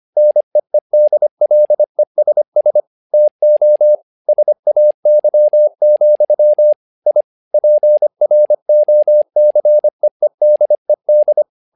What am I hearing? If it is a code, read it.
Morse code, heard as a keyed tone, about 25 wpm, 600 Hz. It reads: NEEDLESS TO SAY, I PROCEEDED
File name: learning center / tower trouble / tower trouble 25wpm